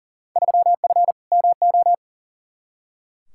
Morse code, heard as a keyed tone, about 40 wpm, 700 Hz. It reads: 3F MO
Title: Field Day Exchanges Encore - 40wpm